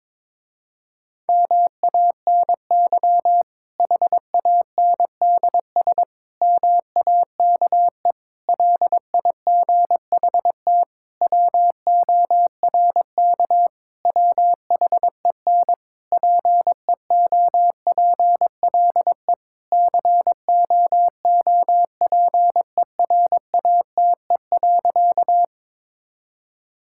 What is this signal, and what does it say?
Morse code, heard as a keyed tone, about 22 wpm, 700 Hz. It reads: MANY HANDS MAKE LIGHT WORK WHEN PEOPLE COOPERATE.
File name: Difficult Sentences with Hints - 22wpm